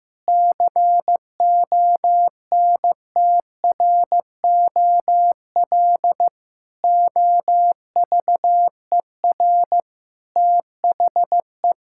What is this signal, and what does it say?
Morse code, heard as a keyed tone, about 15 wpm, 700 Hz. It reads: CONTROL OVER THE
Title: Sentences from Top 1000 Words - Review Part 5 - 15wpm